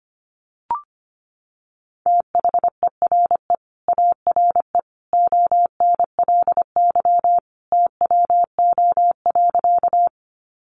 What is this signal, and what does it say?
Morse code, heard as a keyed tone, about 25 wpm, 700 Hz. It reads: THERE ARE ONLY TWO.
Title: Sentences from Top 200 Words 25wpm